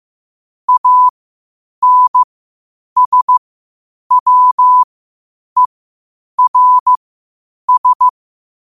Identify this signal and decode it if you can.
Morse code, heard as a keyed tone, about 15 wpm, spaced as for 8 wpm, 1000 Hz. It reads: ANSWERS
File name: 8wpm